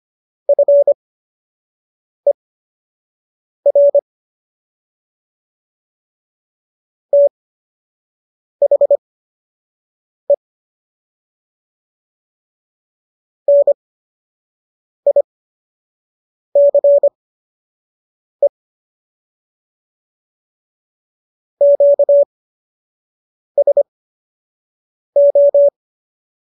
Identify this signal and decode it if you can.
Morse code, heard as a keyed tone, about 25 wpm, spaced as for 6 wpm, 570 Hz. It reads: FER THE NICE QSO